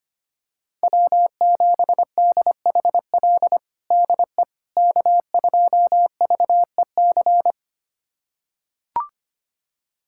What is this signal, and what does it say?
Morse code, heard as a keyed tone, about 25 wpm, 700 Hz. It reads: W7DHL DE K2VEC
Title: USA Callsign DE Callsign - 5 Characters - 25wpm